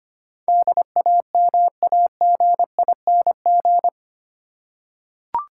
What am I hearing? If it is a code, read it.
Morse code, heard as a keyed tone, about 25 wpm, 700 Hz. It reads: DAMAGING
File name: 8 Letter Words - First 3 Letters Given - 25wpm